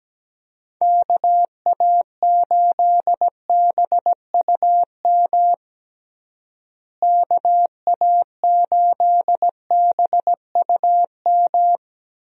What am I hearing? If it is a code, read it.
Morse code, heard as a keyed tone, about 17 wpm, 700 Hz. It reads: KA8BUM KA8BUM
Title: USA Call Signs - 6 Characters - Repeated 3x - 17wpm